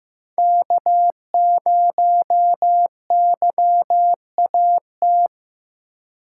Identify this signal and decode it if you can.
Morse code, heard as a keyed tone, about 15 wpm, 700 Hz. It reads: K0YAT
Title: USA Call Signs - 3x Character Spacing - 15wpm